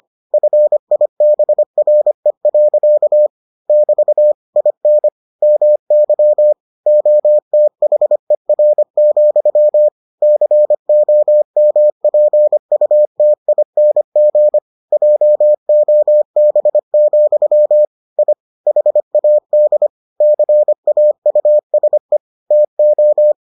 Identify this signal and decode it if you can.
Morse code, heard as a keyed tone, about 25 wpm, 600 Hz. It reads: FIBRE. <BT> IN MY OTHER, COMPUTING JOB, I HAD CAUSE TO